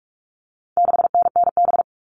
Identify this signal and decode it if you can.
Morse code, heard as a keyed tone, about 44 wpm, 700 Hz. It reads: 6NNB
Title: All Cut Numbers - 4-Digit 45wpm